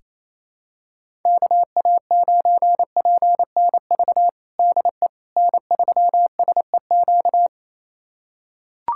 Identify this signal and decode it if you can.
Morse code, heard as a keyed tone, about 28 wpm, 700 Hz. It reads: KA9PNV DE N3SEQ
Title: USA Callsign DE Callsign 28wpm